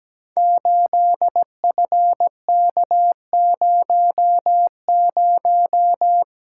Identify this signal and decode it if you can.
Morse code, heard as a keyed tone, about 17 wpm, 700 Hz. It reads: 8FK00
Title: Code Groups - 5 Letter:Numbers - 17wpm